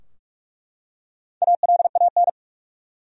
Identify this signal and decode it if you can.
Morse code, heard as a keyed tone, about 43 wpm, 700 Hz. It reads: ALAN
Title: USA Common Names - Max 4 Letters - 45wpm